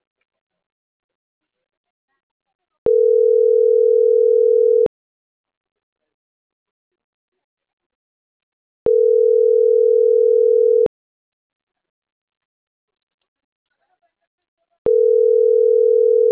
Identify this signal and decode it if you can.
Ringback tone (North America), 440 + 480 Hz.